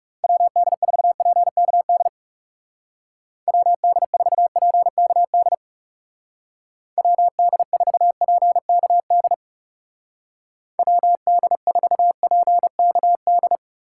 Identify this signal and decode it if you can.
Morse code, heard as a keyed tone, about 36 wpm, 700 Hz. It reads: WD4PKD WD4PKD WD4PKD WD4PKD
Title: Speed Racing - USA Call Signs 30wpm